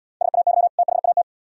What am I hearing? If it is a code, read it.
Morse code, heard as a keyed tone, about 78 wpm, 700 Hz. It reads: SAY THAT